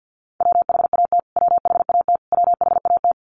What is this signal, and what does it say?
Morse code, heard as a keyed tone, about 53 wpm, 700 Hz. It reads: WHAT WHAT WHAT
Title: Top 500 Words Repeated 50wpm